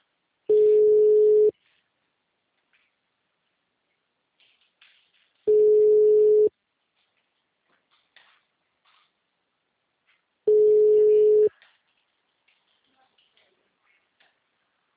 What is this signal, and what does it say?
Ringback tone (Europe, 425 Hz).